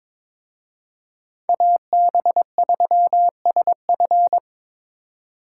Morse code, heard as a keyed tone, about 26 wpm, 700 Hz. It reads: AB3SF
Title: Speed Racing - USA 2x2 Call Signs 22wpm